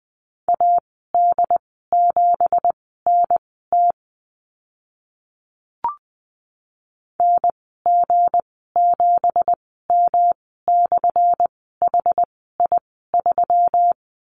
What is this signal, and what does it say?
Morse code, heard as a keyed tone, about 20 wpm, spaced as for 14 wpm, 700 Hz. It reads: AD7NT NG7M/HI3
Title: 186 Lesson 30g - Letter b - Call Signs 20wpm